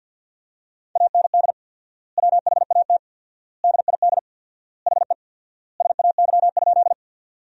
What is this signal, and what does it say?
Morse code, heard as a keyed tone, about 50 wpm, 700 Hz. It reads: AND WHAT DID HE SAY?
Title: 4x Word-Spacing - Sentences from Top 200 Words - 50wpm